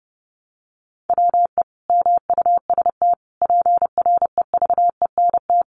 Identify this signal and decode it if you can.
Morse code, heard as a keyed tone, about 30 wpm, 700 Hz. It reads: WE MUST PREVENT
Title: Sentences from Top 1100 Words 30wpm